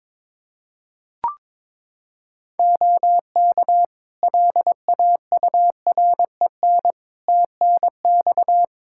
Morse code, heard as a keyed tone, about 21 wpm, 700 Hz. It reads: OK LAUREN TNX
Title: Practice QSO - Exchange 4 - Ending - 22wpm